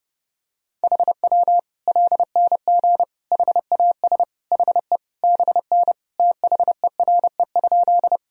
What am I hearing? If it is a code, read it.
Morse code, heard as a keyed tone, about 31 wpm, 700 Hz. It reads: HW LNG HAS HE BN THERE?
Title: Sentences from Top 100 Words and Abbreviations 30wpm